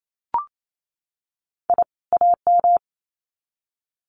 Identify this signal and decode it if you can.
Morse code, heard as a keyed tone, about 28 wpm, 700 Hz. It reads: I AM
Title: Sets of 2 Words 28wpm